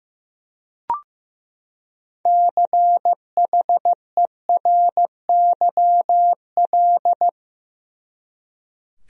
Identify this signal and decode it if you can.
Morse code, heard as a keyed tone, about 15 wpm, 700 Hz. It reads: CHERYL
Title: USA Common Names - 15wpm